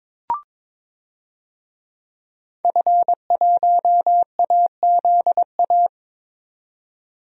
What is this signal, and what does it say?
Morse code, heard as a keyed tone, about 22 wpm, 700 Hz. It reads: F1AZA
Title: DXCC-100 Call Signs - 5 Characters - 22wpm